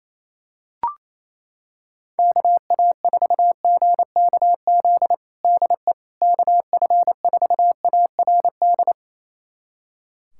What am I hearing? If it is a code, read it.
Morse code, heard as a keyed tone, about 28 wpm, 700 Hz. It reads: KA4GKZ DE KF4ARD
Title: USA Callsign DE Callsign - 6 Characters - 28wpm